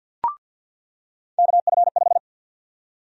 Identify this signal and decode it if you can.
Morse code, heard as a keyed tone, about 50 wpm, 700 Hz. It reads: KF5
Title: USA Call Sign Prefixes - 50wpm